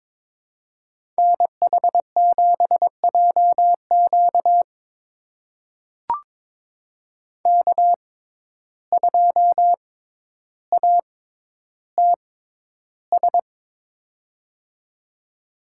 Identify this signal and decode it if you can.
Morse code, heard as a keyed tone, about 22 wpm, spaced as for 11 wpm, 700 Hz. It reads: NH7JQ K2ATS
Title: USA Call Signs - 5 Characters - 6x Character Spacing - 22wpm